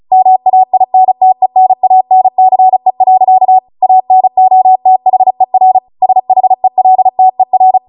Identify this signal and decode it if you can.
Morse code, heard as a keyed tone, about 36 wpm, 750 Hz. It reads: MAINTENANCE. ANOTHER SHELTER
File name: Morse / 35WPM